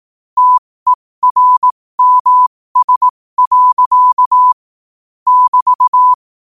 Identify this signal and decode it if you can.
Morse code, heard as a keyed tone, about 18 wpm, spaced as for 15 wpm, 1000 Hz. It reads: TERMS. <BT>